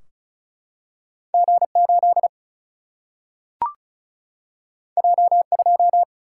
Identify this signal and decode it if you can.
Morse code, heard as a keyed tone, about 36 wpm, 700 Hz. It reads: G8 J2
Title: Single-Letter Single-Number 35wpm